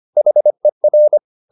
Morse code, heard as a keyed tone, about 25 wpm, 600 Hz. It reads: HER